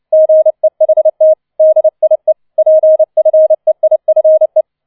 Morse code, heard as a keyed tone, about 29 wpm, 630 Hz. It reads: GEHT DIE PFEIFE